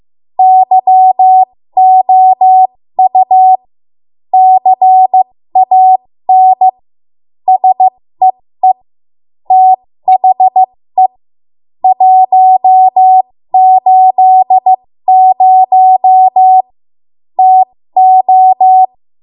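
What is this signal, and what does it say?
Morse code, heard as a keyed tone, about 15 wpm, spaced as for 13 wpm, 750 Hz. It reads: YOU CAN SEE THE 180 TO
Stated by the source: 13 wpm Morse code practice archive transmission.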